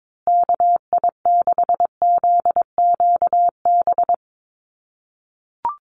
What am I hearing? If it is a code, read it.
Morse code, heard as a keyed tone, about 22 wpm, 700 Hz. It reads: KI6ZQB
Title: USA Call Signs Repeated 2x 22wpm